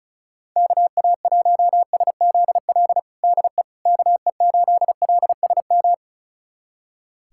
Morse code, heard as a keyed tone, about 36 wpm, 700 Hz. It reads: KA1SZL DE KE8LSM
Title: USA Callsign DE Callsign - 6 Characters - 35wpm